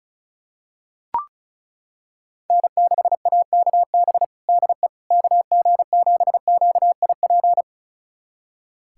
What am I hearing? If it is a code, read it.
Morse code, heard as a keyed tone, about 36 wpm, 700 Hz. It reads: N6AKB DE KG7QIP
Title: USA Callsign DE Callsign 35wpm